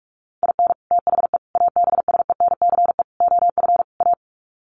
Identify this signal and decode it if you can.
Morse code, heard as a keyed tone, about 44 wpm, 700 Hz. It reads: IN THE ABSENCE OF A
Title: Sets of 5 Words Encore 45wpm